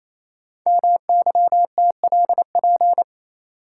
Morse code, heard as a keyed tone, about 27 wpm, 700 Hz. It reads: MYTLP
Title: Code Groups - 5 Letters - 28wpm